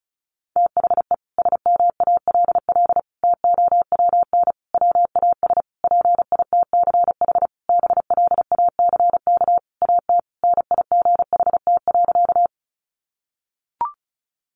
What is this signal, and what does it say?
Morse code, heard as a keyed tone, about 34 wpm, 700 Hz. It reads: THE SMALL TOWN WAS PITCH BLACK AT NIGHT.
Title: Sentences from Top 1300 Words 35wpm